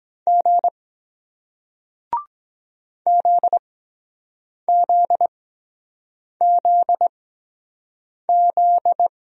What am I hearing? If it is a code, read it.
Morse code, heard as a keyed tone, about 23 wpm, spaced as for 7 wpm, 700 Hz. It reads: G ZZZZ